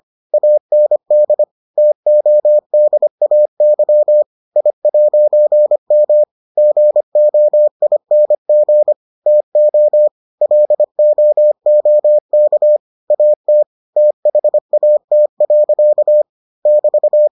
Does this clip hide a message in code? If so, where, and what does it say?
Yes: Morse code from 0.3 to 17.3 s, heard as a keyed tone, about 25 wpm, 600 Hz. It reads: AND TODAY I'M GOING TO LOOK AT THAT. <BT>